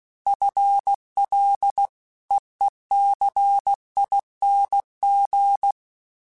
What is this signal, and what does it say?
Morse code, heard as a keyed tone, about 16 wpm, 800 Hz. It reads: FLEECING
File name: FL / FL mp3s / FL Prefixes 3x2 sp @15